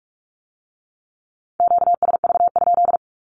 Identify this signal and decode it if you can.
Morse code, heard as a keyed tone, about 46 wpm, 700 Hz. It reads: QSV?
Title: Q-Codes with Spoken Meaning - 45wpm